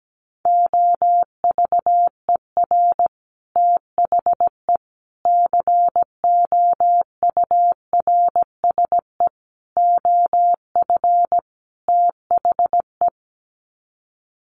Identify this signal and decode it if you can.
Morse code, heard as a keyed tone, about 17 wpm, 700 Hz. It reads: OVER THE COURSE OF THE
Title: Sets of 5 Words 17wpm